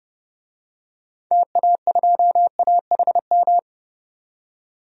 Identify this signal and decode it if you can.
Morse code, heard as a keyed tone, about 30 wpm, 700 Hz. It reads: TA2AHM
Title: DXCC-100 Call Signs - 6 Characters - 30wpm